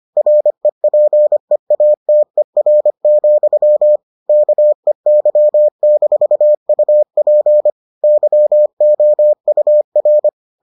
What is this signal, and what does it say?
Morse code, heard as a keyed tone, about 25 wpm, 600 Hz. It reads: REPEATER, KEY-UP YOUR